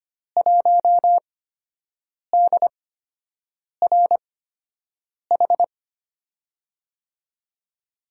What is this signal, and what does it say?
Morse code, heard as a keyed tone, about 25 wpm, spaced as for 7 wpm, 700 Hz. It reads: 1DRH
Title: USA Call Signs - 6 Characters - 8x Character Spacing - 25wpm